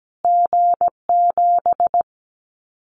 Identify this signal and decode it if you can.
Morse code, heard as a keyed tone, about 17 wpm, 700 Hz. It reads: G7
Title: Callsign Prefixes for Top 25 DXCC - Repeated 3x - 17wpm